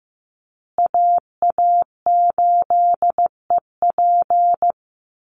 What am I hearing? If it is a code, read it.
Morse code, heard as a keyed tone, about 15 wpm, 700 Hz. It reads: AA8EP
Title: USA Call Signs Repeated 3x 15wpm